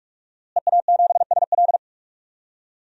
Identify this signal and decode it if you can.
Morse code, heard as a keyed tone, about 46 wpm, 700 Hz. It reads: EA7SL
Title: DXCC-100 Call Signs - 5 Characters - Repeated 3x - 45wpm